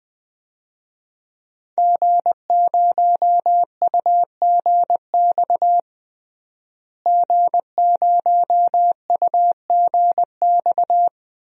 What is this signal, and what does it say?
Morse code, heard as a keyed tone, about 20 wpm, 700 Hz. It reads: G0UGX G0UGX
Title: DXCC-100 Call Signs - 5 Characters - Repeated 2x - 20wpm